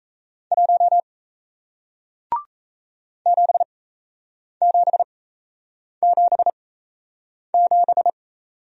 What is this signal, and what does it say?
Morse code, heard as a keyed tone, about 37 wpm, spaced as for 8 wpm, 700 Hz. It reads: 1 7777